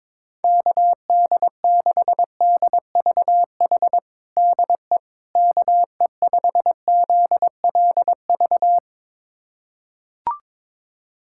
Morse code, heard as a keyed tone, about 22 wpm, 700 Hz. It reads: KD6DVH DE KE5ZLV
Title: USA Callsign DE Callsign 22wpm